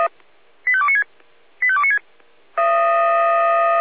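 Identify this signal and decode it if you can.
Five-tone selective call (ZVEI), heard as quick runs of five tones, tones of about 70 ms. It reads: 85296 85296
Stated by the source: ZVEI five-tone call received by SDR.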